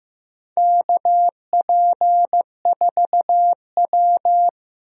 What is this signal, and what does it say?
Morse code, heard as a keyed tone, about 15 wpm, 700 Hz. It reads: KP4W
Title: DXCC-100 Call Signs - 4 Characters - 15wpm